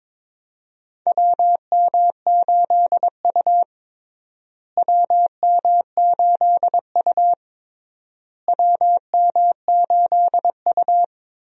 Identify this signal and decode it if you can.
Morse code, heard as a keyed tone, about 22 wpm, 700 Hz. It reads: WM8U WM8U WM8U
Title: USA Call Signs - 4 Characters - Repeated 3x - 22wpm